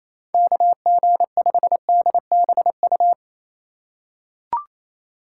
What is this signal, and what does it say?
Morse code, heard as a keyed tone, about 27 wpm, 700 Hz. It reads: KG5DBU
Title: USA Call Signs - 6 Characters - 3x Character Spacing - 28wpm